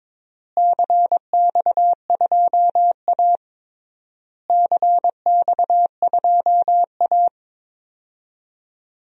Morse code, heard as a keyed tone, about 22 wpm, 700 Hz. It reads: CX2A CX2A
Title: DXCC-100 Call Signs - 4 Characters - Repeated 3x - 22wpm